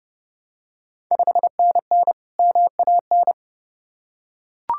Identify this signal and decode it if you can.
Morse code, heard as a keyed tone, about 32 wpm, 700 Hz. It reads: 5NN MAN
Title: Ontario QSO Party Exchange 30wpm